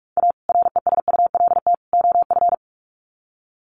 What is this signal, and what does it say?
Morse code, heard as a keyed tone, about 44 wpm, 700 Hz. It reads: A RESULT OF
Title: Sets of 3 Words 45wpm